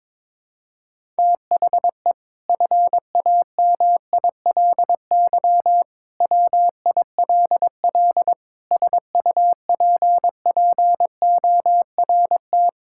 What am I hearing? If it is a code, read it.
Morse code, heard as a keyed tone, about 22 wpm, 700 Hz. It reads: THE FAMILY WILL SUPPORT
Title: Sentences from Top 1000 Words - Review Part 5 - 22wpm